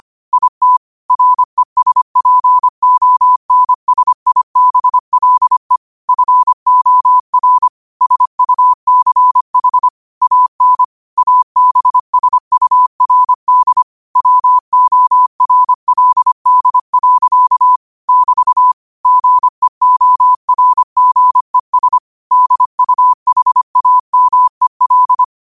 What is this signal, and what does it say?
Morse code, heard as a keyed tone, about 25 wpm, 1000 Hz. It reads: IT RESPONSIBLE FOR SUCH AN ABSURD WORLD. <BT> GEORGES DUHAMEL
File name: qotdfiles / 25wpm